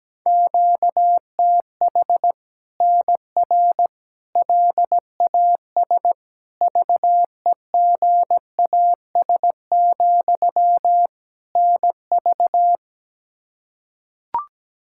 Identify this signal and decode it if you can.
Morse code, heard as a keyed tone, about 17 wpm, 700 Hz. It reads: QTH NR LAS VEGAS, NV
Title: Practice QSO - Exchange 1 - Beginning - 17wpm